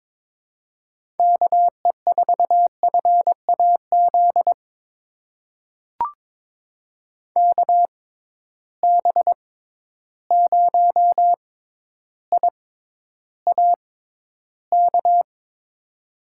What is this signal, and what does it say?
Morse code, heard as a keyed tone, about 22 wpm, spaced as for 11 wpm, 700 Hz. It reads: KE4FAZ KB0IAK
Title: USA Call Signs - 6 Characters - 6x Character Spacing - 22wpm